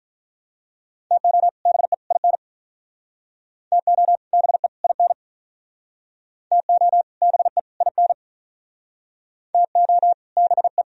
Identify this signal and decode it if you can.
Morse code, heard as a keyed tone, about 45 wpm, 700 Hz. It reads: TO BE IN TO BE IN TO BE IN TO BE